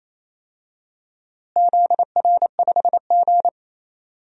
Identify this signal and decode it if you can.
Morse code, heard as a keyed tone, about 28 wpm, 700 Hz. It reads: ZR5G
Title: DXCC-100 Call Signs - 4 Characters - Repeated 3x - 28wpm